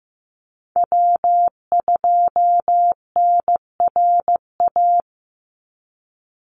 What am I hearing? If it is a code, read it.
Morse code, heard as a keyed tone, about 15 wpm, 700 Hz. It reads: W2NRA
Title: Call Signs Repeated 2x - 15wpm